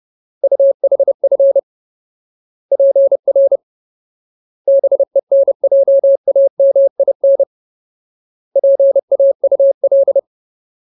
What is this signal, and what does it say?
Morse code, heard as a keyed tone, about 30 wpm, 550 Hz. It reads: UHF PR BENJAMIN PAUL
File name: Ultimate Word List / 30WPM Ultimate